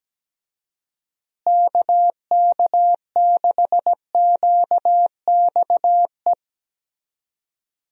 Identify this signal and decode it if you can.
Morse code, heard as a keyed tone, about 17 wpm, 700 Hz. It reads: KK6QXE
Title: USA Call Signs - 6 Characters - Repeated 2x - 17wpm